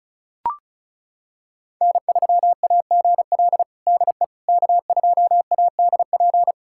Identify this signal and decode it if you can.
Morse code, heard as a keyed tone, about 36 wpm, 700 Hz. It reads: N3AGL DE K2ADP
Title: USA Callsign DE Callsign - 5 Characters - 35wpm